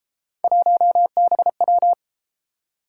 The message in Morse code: 1BW

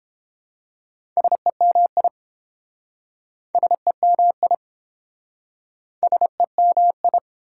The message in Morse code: SEMI SEMI SEMI

